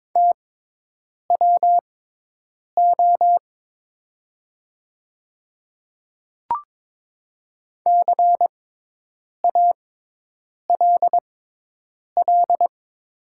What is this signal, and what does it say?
Morse code, heard as a keyed tone, about 22 wpm, spaced as for 8 wpm, 700 Hz. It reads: TWO CALL
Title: 6x Character Spacing - Top 200 Words - 22wpm